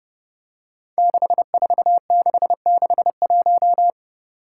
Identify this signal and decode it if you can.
Morse code, heard as a keyed tone, about 30 wpm, 700 Hz. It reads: 64661
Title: Five Challenging Letter-Number Combinations 30wpm